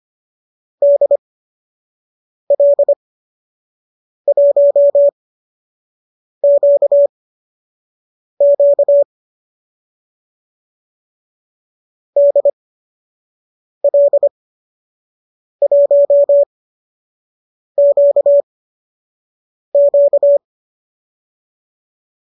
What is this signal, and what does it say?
Morse code, heard as a keyed tone, about 25 wpm, spaced as for 6 wpm, 590 Hz. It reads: DL1QQ DL1QQ
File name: qso012 06